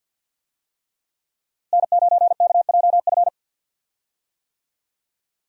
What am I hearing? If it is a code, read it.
Morse code, heard as a keyed tone, about 52 wpm, 700 Hz. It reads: N9KJF
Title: USA Call Signs - 5 Characters - Repeated 3x - 50wpm